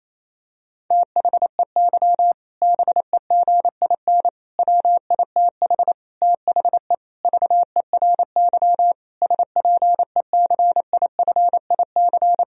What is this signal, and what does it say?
Morse code, heard as a keyed tone, about 28 wpm, 700 Hz. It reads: THEY BEGIN WITH THE VERY SPECIFIC
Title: Sentences from Top 1000 Words - Review Part 3 - 28wpm